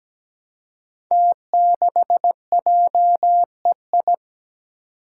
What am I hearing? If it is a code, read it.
Morse code, heard as a keyed tone, about 17 wpm, 700 Hz. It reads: T6JEI
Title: Five Challenging Letter-Number Combinations 17wpm